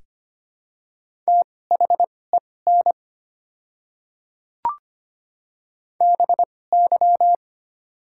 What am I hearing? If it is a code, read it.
Morse code, heard as a keyed tone, about 25 wpm, spaced as for 18 wpm, 700 Hz. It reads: THEN BY
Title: 2x Character Spacing - Top 100 Words - 25wpm